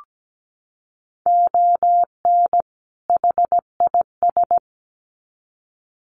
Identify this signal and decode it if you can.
Morse code, heard as a keyed tone, about 17 wpm, 700 Hz. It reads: ON HIS
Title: Sets of 2 Words 17wpm